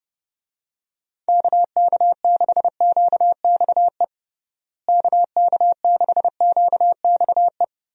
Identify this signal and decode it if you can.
Morse code, heard as a keyed tone, about 30 wpm, 700 Hz. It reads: KK6QXE KK6QXE
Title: USA Call Signs - 6 Characters - Repeated 2x - 30wpm